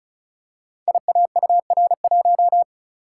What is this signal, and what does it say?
Morse code, heard as a keyed tone, about 36 wpm, 700 Hz. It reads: IAUR1
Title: Code Groups - 5 Letter:Numbers - 35wpm